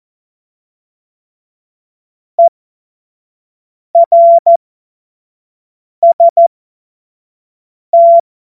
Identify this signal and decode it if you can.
Morse code, heard as a keyed tone, about 14 wpm, spaced as for 5 wpm, 680 Hz. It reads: ERST
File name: Morsekurs/Witze